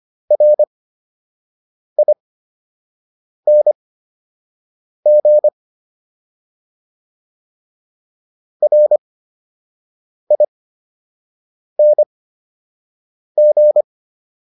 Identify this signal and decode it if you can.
Morse code, heard as a keyed tone, about 25 wpm, spaced as for 6 wpm, 600 Hz. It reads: RING RING